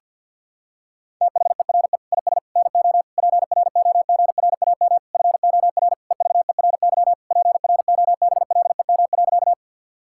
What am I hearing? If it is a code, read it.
Morse code, heard as a keyed tone, about 46 wpm, 700 Hz. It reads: THERE IS NO PROGRAM FOR EVERY PROBLEM.
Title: Sentences from Top 1000 Words - Review Part 5 - 50wpm